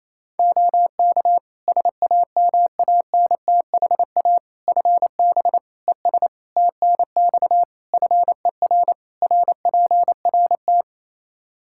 Morse code, heard as a keyed tone, about 28 wpm, 700 Hz. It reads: OK SAMANTHA FB ES TNX FER RPRT
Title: Practice QSO - Exchange 2 - Info - 28wpm